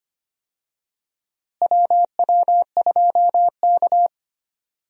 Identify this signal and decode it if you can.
Morse code, heard as a keyed tone, about 25 wpm, 700 Hz. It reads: WW2K
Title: USA Call Signs - 4 Characters - Repeated 3x - 25wpm